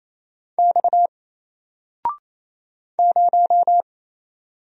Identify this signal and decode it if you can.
Morse code, heard as a keyed tone, about 27 wpm, 700 Hz. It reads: X 0
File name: Single-Letter-Number-28wpm